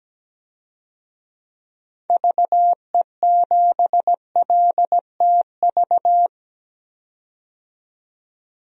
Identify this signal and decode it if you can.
Morse code, heard as a keyed tone, about 17 wpm, 700 Hz. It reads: VE7LTV
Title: DXCC-100 Call Signs - 6 Characters - 17wpm